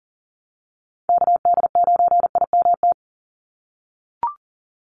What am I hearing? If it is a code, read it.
Morse code, heard as a keyed tone, about 40 wpm, 700 Hz. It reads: KD9IMT
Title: USA Call Signs 40wpm